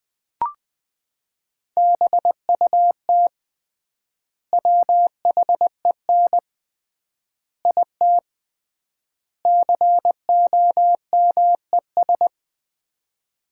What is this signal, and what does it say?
Morse code, heard as a keyed tone, about 20 wpm, 700 Hz. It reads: BUT WHEN IT COMES
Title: 3x Word-Spacing - Sets of 5 Words - 20wpm